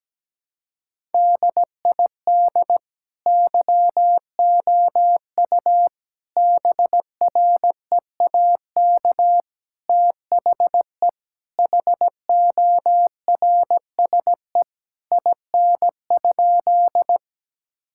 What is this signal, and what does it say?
Morse code, heard as a keyed tone, about 17 wpm, 700 Hz. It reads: DID YOU BREAK THE HORSE IN?
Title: Sentences from Top 1000 Words - Review Part 6 - 17wpm